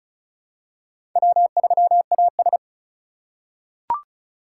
Morse code, heard as a keyed tone, about 36 wpm, 700 Hz. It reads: W3AS
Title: USA Call Signs - 4 Characters - 4x Character Spacing - 35wpm